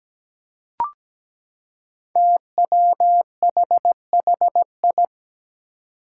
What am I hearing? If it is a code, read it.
Morse code, heard as a keyed tone, about 17 wpm, 700 Hz. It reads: TWHHI
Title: Five Challenging Letter-Number Combinations 17wpm